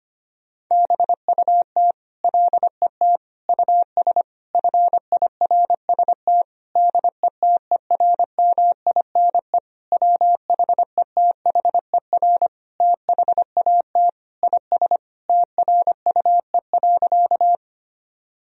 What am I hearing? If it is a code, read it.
Morse code, heard as a keyed tone, about 25 wpm, 700 Hz. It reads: BUT LET US FIRST DETERMINE WHETHER THAT IS TRUE.
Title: Sentences from Top 1000 Words - Review Part 4 - 25wpm